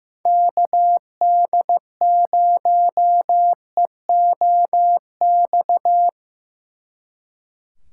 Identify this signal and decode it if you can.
Morse code, heard as a keyed tone, about 17 wpm, 700 Hz. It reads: KD0EOX
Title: Speed Racing - USA Call Signs 15wpm